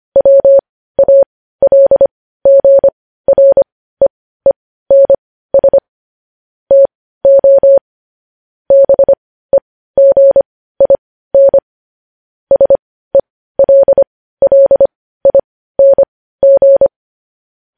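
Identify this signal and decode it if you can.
Morse code, heard as a keyed tone, about 25 wpm, spaced as for 15 wpm, 550 Hz. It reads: WALGREENS TO BEGIN SELLING
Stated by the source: Morse code news-headline practice sent at 15 words per minute.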